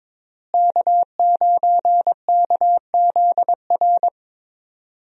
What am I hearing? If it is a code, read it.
Morse code, heard as a keyed tone, about 22 wpm, 700 Hz. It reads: K9KZR
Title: USA Call Signs - 5 Characters - Repeated 2x - 22wpm